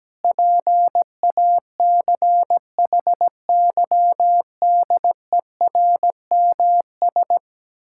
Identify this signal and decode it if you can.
Morse code, heard as a keyed tone, about 17 wpm, 700 Hz. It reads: PACHYDERMS
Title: Pangram Challenge v2 - 17wpm